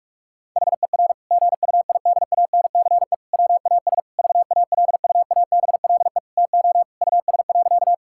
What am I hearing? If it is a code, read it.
Morse code, heard as a keyed tone, about 41 wpm, 700 Hz. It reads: HER GUIDANCE WAS VALUABLE TO US.
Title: Sentences from Top 1600 Words 45wpm